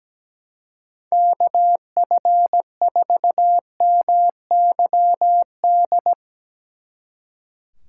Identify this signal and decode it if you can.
Morse code, heard as a keyed tone, about 18 wpm, 700 Hz. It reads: KF4MYD
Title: Speed Racing - USA 2x3 Call Signs 17wpm